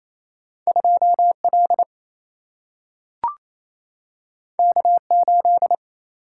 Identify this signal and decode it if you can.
Morse code, heard as a keyed tone, about 28 wpm, 700 Hz. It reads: 2 L K 8
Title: Single-Letter Single-Number 28wpm